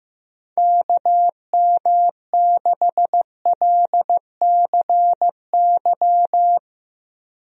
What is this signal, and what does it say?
Morse code, heard as a keyed tone, about 15 wpm, 700 Hz. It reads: KM6LCY
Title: USA Call Signs - 6 Characters - Repeated 2x - 15wpm